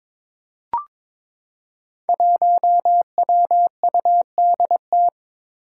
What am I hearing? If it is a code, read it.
Morse code, heard as a keyed tone, about 22 wpm, 700 Hz. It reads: 1WUDT
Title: Five Challenging Letter-Number Combinations 22wpm